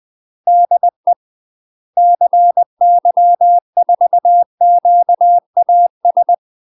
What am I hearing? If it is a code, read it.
Morse code, heard as a keyed tone, about 20 wpm, 700 Hz. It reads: DE CY4QAS
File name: UK MASTER QSOs TRAINING / 20WPM / UK MASTER QSOs 0046